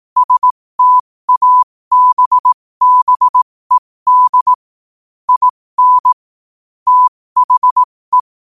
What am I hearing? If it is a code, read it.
Morse code, heard as a keyed tone, about 18 wpm, spaced as for 15 wpm, 1000 Hz. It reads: STABBED IN THE
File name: qotdfiles / 15wpm